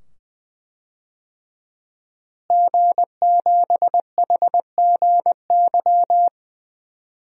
Morse code, heard as a keyed tone, about 20 wpm, 700 Hz. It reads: G7HGY
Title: DXCC-100 Call Signs - 5 Characters - Repeated 2x - 20wpm